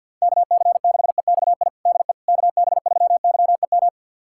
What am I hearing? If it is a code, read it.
Morse code, heard as a keyed tone, about 52 wpm, 700 Hz. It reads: KC6EXI DE KB3YEM
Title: USA Callsign DE Callsign 50wpm